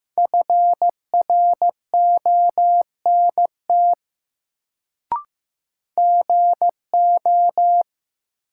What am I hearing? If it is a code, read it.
Morse code, heard as a keyed tone, about 15 wpm, 700 Hz. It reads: FRONT GO
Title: Sets of 2 Words - New Word Drill - 15wpm